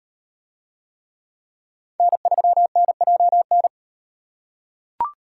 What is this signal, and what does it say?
Morse code, heard as a keyed tone, about 36 wpm, 700 Hz. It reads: N3NJN